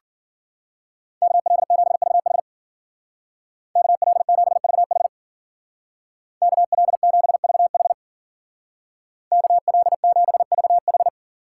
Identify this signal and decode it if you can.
Morse code, heard as a keyed tone, about 48 wpm, 700 Hz. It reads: KL7VH KL7VH KL7VH KL7VH
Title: Speed Racing - USA 2x2 Call Signs 40wpm